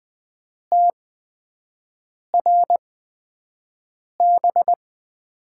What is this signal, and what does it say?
Morse code, heard as a keyed tone, about 20 wpm, spaced as for 5 wpm, 700 Hz. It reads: TRB
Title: USA Call Signs - 6 Characters - 8x Character Spacing - 20wpm